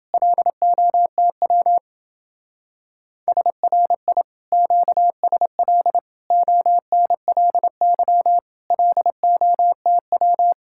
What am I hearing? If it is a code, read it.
Morse code, heard as a keyed tone, about 28 wpm, 700 Hz. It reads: LOTW SRI QSL ONLY LOTW